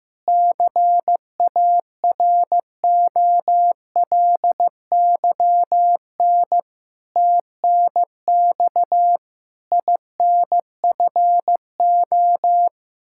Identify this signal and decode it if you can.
Morse code, heard as a keyed tone, about 15 wpm, 700 Hz. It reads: CAROLYN TNX INFO
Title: Practice QSO - Exchange 3 - More Info - 15wpm